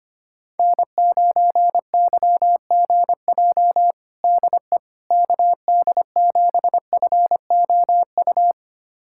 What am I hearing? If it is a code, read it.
Morse code, heard as a keyed tone, about 25 wpm, 700 Hz. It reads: N9YGJ DE KD7FOU
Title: USA Callsign DE Callsign 25wpm